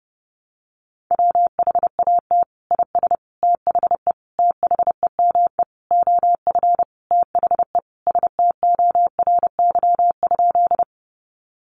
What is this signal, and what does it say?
Morse code, heard as a keyed tone, about 30 wpm, 700 Hz. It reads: WHAT IS THE THEME OF THE STORY?
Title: Sentences from Top 1100 Words 30wpm